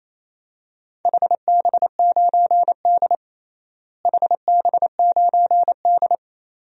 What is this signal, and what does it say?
Morse code, heard as a keyed tone, about 28 wpm, 700 Hz. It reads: HB9D HB9D
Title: DXCC-100 Call Signs - 4 Characters - Repeated 2x - 28wpm